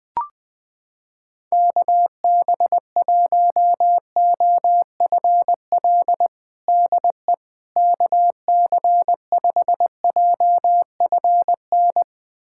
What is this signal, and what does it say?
Morse code, heard as a keyed tone, about 21 wpm, 700 Hz. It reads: KB1OFL DE KC5JFN
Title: USA Callsign DE Callsign - 6 Characters - 20wpm